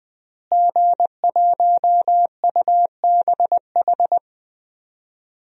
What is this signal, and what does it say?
Morse code, heard as a keyed tone, about 21 wpm, 700 Hz. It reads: G1UBH
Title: DXCC-100 Call Signs - 5 Characters - Repeated 3x - 20wpm